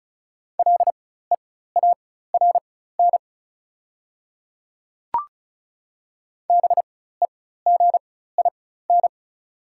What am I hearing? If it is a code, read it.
Morse code, heard as a keyed tone, about 35 wpm, spaced as for 16 wpm, 700 Hz. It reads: LEARN BEGIN
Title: 4x Character Spacing - Top 300 Words - 35wpm